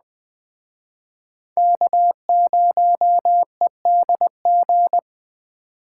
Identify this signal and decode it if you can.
Morse code, heard as a keyed tone, about 21 wpm, 700 Hz. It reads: K0EDG
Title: Speed Racing - USA 1x3 Call Signs 17wpm